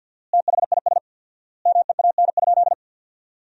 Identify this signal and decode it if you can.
Morse code, heard as a keyed tone, about 50 wpm, 700 Hz. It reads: THIS MEAN?
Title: 4x Word-Spacing - Sentences from Top 200 Words - 50wpm